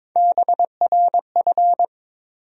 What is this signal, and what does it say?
Morse code, heard as a keyed tone, about 21 wpm, 700 Hz. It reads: BRF
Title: USA Call Sign Postfixes - 3 Letters 22wpm